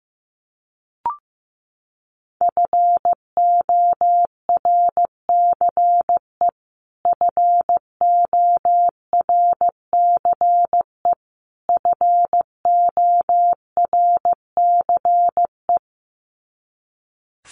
Morse code, heard as a keyed tone, about 15 wpm, 700 Hz. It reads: FORCE FORCE FORCE
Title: Top 500 Words Repeated 15wpm